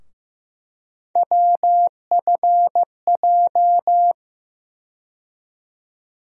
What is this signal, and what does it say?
Morse code, heard as a keyed tone, about 15 wpm, 700 Hz. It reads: WFJ
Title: USA Call Sign Postfixes - 3 Letters 15wpm